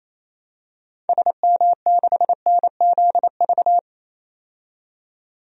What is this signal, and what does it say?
Morse code, heard as a keyed tone, about 28 wpm, 700 Hz. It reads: SM6NZV
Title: DXCC-100 Call Signs - 6 Characters - 28wpm